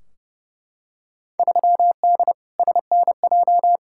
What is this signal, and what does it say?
Morse code, heard as a keyed tone, about 31 wpm, 700 Hz. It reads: 3D SNJ
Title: Field Day Exchanges Encore - 30wpm